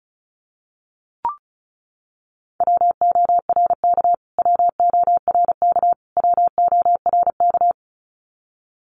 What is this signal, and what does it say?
Morse code, heard as a keyed tone, about 34 wpm, 700 Hz. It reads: WORK WORK WORK
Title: Top 500 Words Repeated 35wpm